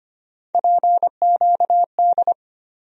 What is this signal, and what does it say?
Morse code, heard as a keyed tone, about 25 wpm, 700 Hz. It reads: PQD
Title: USA Call Sign Postfixes - 3 Letters 25wpm